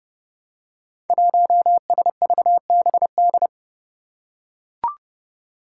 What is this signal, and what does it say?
Morse code, heard as a keyed tone, about 30 wpm, 700 Hz. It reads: 1SVBD
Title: Five Challenging Letter-Number Combinations 30wpm